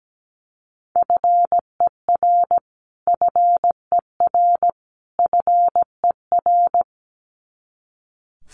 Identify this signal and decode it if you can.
Morse code, heard as a keyed tone, about 17 wpm, 700 Hz. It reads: FER FER FER
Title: Abbreviations Repeated 3x - 17wpm